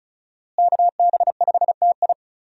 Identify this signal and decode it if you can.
Morse code, heard as a keyed tone, about 36 wpm, 700 Hz. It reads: KB5TI
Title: USA Call Signs - 5 Characters - Repeated 2x - 35wpm